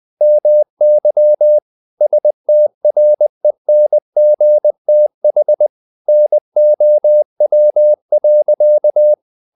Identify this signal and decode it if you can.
Morse code, heard as a keyed tone, about 20 wpm, 600 Hz. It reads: MY STRENGTH NOW.
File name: the old man and the sea / the old man and the sea part 1 20wpm